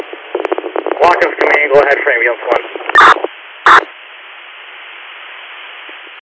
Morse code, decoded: none

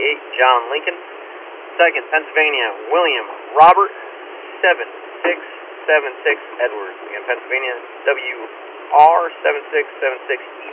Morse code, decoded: none

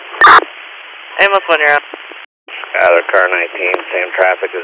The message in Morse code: none

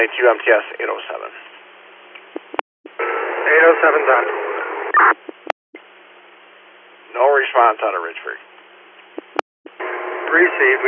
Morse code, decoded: none

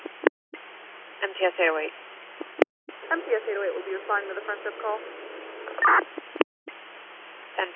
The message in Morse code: none

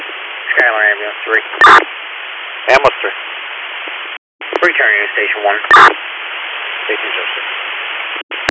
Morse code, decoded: none

